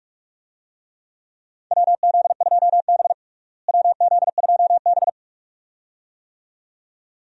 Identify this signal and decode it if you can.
Morse code, heard as a keyed tone, about 46 wpm, 700 Hz. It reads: WZ2B WZ2B
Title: USA Call Signs - 4 Characters - Repeated 2x - 45wpm